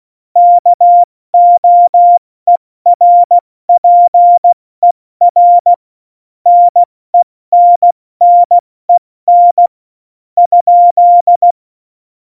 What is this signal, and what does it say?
Morse code, heard as a keyed tone, about 16 wpm, 700 Hz. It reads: KOERPER NENNEN ?
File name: Morsekurs / Witze / Witz 02. April 2019, 70 F80